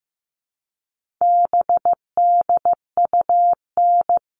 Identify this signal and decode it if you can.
Morse code, heard as a keyed tone, about 15 wpm, 700 Hz. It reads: BDUN
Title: All Cut Numbers - 4-Digit 15wpm